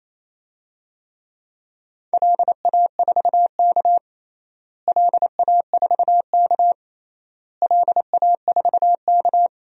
Morse code, heard as a keyed tone, about 27 wpm, 700 Hz. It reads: LA4K LA4K LA4K
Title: DXCC-100 Call Signs - 4 Characters - Repeated 3x - 28wpm